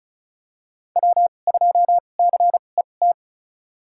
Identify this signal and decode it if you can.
Morse code, heard as a keyed tone, about 34 wpm, spaced as for 25 wpm, 700 Hz. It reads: W2CET
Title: USA Call Signs - 5 Characters - 2x Character Spacing - 35wpm